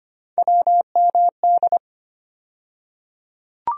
Morse code, heard as a keyed tone, about 25 wpm, 700 Hz. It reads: WMD
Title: USA Call Sign Postfixes - 3 Letters 25wpm